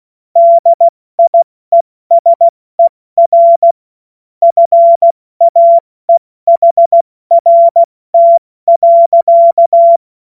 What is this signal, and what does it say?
Morse code, heard as a keyed tone, about 16 wpm, 680 Hz. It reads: DIESER FAEHRT.
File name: Witz-26-August-2025-70-F80